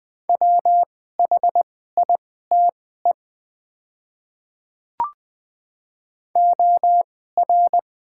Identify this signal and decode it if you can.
Morse code, heard as a keyed tone, about 21 wpm, spaced as for 15 wpm, 700 Hz. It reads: WHITE OR
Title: 2x Character Spacing - Top 300 Words - 20wpm